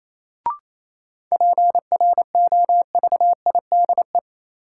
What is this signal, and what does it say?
Morse code, heard as a keyed tone, about 28 wpm, 700 Hz. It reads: PROVIDE
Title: Sentences from Top 1000 Words - Review Part 2 - New Word Drill - 28wpm